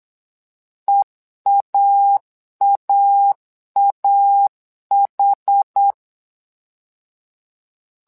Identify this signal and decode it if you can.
Morse code, heard as a keyed tone, about 8 wpm, 800 Hz. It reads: EAAAH
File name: LEZIONI / 2 UHDN / 2 UH 8wpm 2129